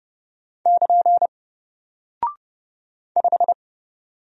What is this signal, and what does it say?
Morse code, heard as a keyed tone, about 29 wpm, 700 Hz. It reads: <KN> 5
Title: Single Character (Letter:Number:Punctuation:Prosign) 30wpm